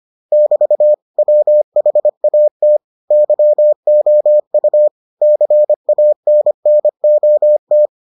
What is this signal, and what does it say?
Morse code, heard as a keyed tone, about 25 wpm, 600 Hz. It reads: <BT> WHAT YOU CANNOT